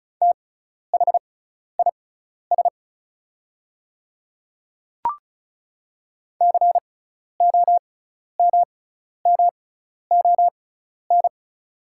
Morse code, heard as a keyed tone, about 36 wpm, spaced as for 12 wpm, 700 Hz. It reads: THIS COMMON
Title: 6x Character Spacing - Top 500 Words - 35wpm